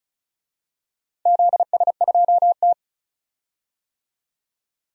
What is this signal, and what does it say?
Morse code, heard as a keyed tone, about 36 wpm, 700 Hz. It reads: ZS2T
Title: DXCC-100 Call Signs - 4 Characters - 35wpm